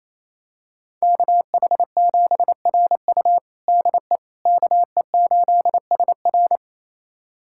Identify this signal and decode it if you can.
Morse code, heard as a keyed tone, about 28 wpm, 700 Hz. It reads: KH7RU DE KE8SR
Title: USA Callsign DE Callsign - 5 Characters - 28wpm